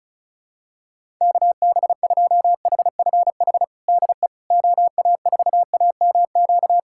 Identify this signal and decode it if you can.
Morse code, heard as a keyed tone, about 34 wpm, 700 Hz. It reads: KB2HFH DE OA4AMQ
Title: USA Callsign DE Callsign - 6 Characters - 35wpm